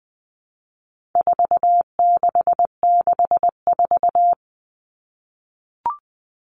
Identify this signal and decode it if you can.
Morse code, heard as a keyed tone, about 20 wpm, 700 Hz. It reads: 4664
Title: All Cut Numbers - 4-Digit 20wpm